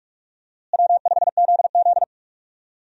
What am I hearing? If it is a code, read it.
Morse code, heard as a keyed tone, about 46 wpm, 700 Hz. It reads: W5ZZ
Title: USA Call Signs - 4 Characters - Repeated 3x - 45wpm